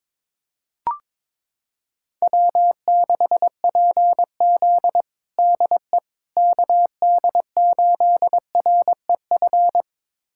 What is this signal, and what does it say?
Morse code, heard as a keyed tone, about 22 wpm, 700 Hz. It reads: W6PZ DE KD8REF
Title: USA Callsign DE Callsign 22wpm